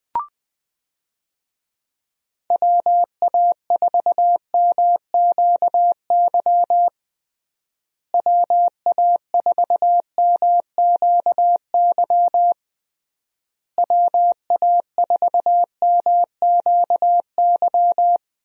Morse code, heard as a keyed tone, about 20 wpm, 700 Hz. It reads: WA4MQY WA4MQY WA4MQY
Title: USA Call Signs - 6 Characters - Repeated 3x - 20wpm